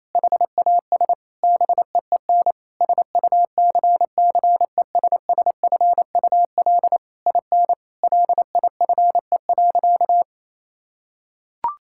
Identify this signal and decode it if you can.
Morse code, heard as a keyed tone, about 28 wpm, 700 Hz. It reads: HAS BEEN SUCCESSFUL IN LIFE.
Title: Sentences from Top 1000 Words - Review Part 6 - 28wpm